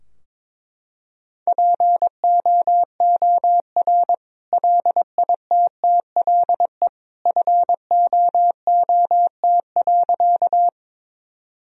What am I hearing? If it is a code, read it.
Morse code, heard as a keyed tone, about 22 wpm, 700 Hz. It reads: POOR LITTLE FOOT.
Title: Sentences from Top 1000 Words - Review Part 3 - 22wpm